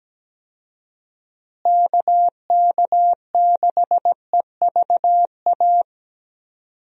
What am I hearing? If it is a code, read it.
Morse code, heard as a keyed tone, about 17 wpm, 700 Hz. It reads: KK6EVA
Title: USA Call Signs - 6 Characters - Repeated 2x - 17wpm